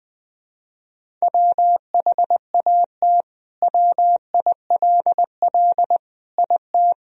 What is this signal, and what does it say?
Morse code, heard as a keyed tone, about 20 wpm, 700 Hz. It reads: WHAT WILL IT
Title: Sentences from Top 1000 Words - Review Part 3 - 20wpm